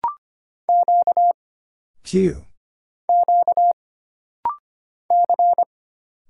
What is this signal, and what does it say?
Morse code, heard as a keyed tone, about 25 wpm, 700 Hz. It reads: Q Q C
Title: Single Letters - Rapid-Fire 25wpm